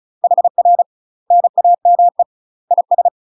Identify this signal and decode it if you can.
Morse code, heard as a keyed tone, about 34 wpm, 700 Hz. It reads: HR NAME IS
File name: UK MASTER QSOs TRAINING/35WPM